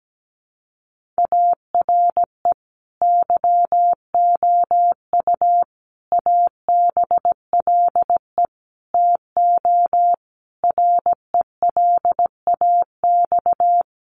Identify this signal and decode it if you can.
Morse code, heard as a keyed tone, about 17 wpm, 700 Hz. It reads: ARE YOU ABLE TO RELAX
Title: Sentences from Top 1400 Words 17wpm